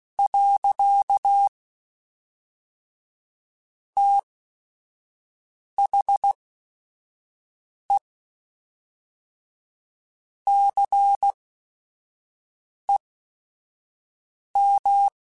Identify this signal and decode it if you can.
Morse code, heard as a keyed tone, about 16 wpm, spaced as for 5 wpm, 800 Hz. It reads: . THE CEM
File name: Technical Subjects / Technical Subjects-mp3s / XL @5